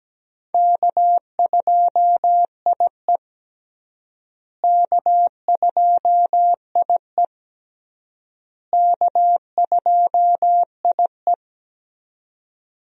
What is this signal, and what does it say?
Morse code, heard as a keyed tone, about 17 wpm, 700 Hz. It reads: K2IE K2IE K2IE
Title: USA Call Signs - 4 Characters - Repeated 3x - 17wpm